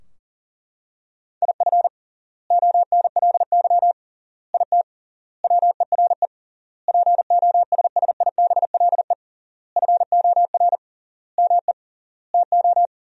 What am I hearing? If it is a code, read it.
Morse code, heard as a keyed tone, about 40 wpm, 700 Hz. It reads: IF ONLY IT WERE POSSIBLE FOR ME TO
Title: 3x Word-Spacing - Sentences from Top 500 Words - 40wpm